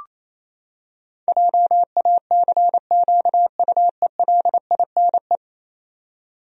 Morse code, heard as a keyed tone, about 28 wpm, 700 Hz. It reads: JACQUELINE
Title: USA Common Names - 28wpm